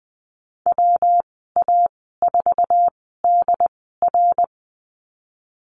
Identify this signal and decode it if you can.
Morse code, heard as a keyed tone, about 20 wpm, spaced as for 15 wpm, 700 Hz. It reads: WA4DR
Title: 186 Lesson 30g - Letter b - Call Signs 20wpm